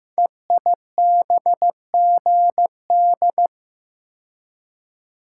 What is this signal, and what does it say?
Morse code, heard as a keyed tone, about 15 wpm, 700 Hz. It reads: EIBGD
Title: Code Groups - 5 Letters - 15wpm